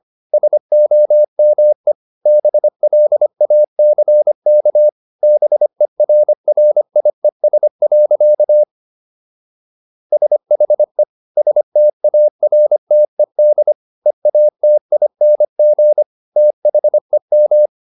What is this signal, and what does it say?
Morse code, heard as a keyed tone, about 25 wpm, 600 Hz. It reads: SOME BLACK BERRIES. SHE STARTED EATING THEM